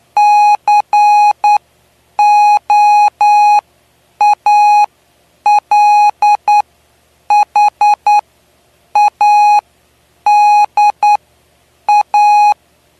Morse code, heard as a keyed tone, about 10 wpm, spaced as for 7 wpm, 830 Hz. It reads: COALHADA
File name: cw / py2lcd / 09-PY2LCD-Li��o 9